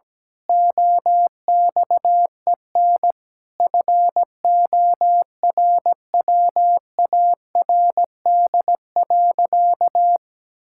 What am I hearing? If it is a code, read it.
Morse code, heard as a keyed tone, about 17 wpm, 700 Hz. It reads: OXEN FORWARD.
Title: Pangram Challenge v2 - 17wpm